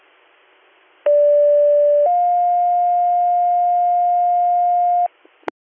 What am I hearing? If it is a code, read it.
Two-tone page, heard as a one-second tone, then a three-second tone, about 580 then 710 Hz. It is heard from 1.1 to 5.1 s.